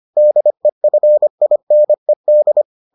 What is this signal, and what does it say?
Morse code, heard as a keyed tone, about 25 wpm, 600 Hz. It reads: DEFINED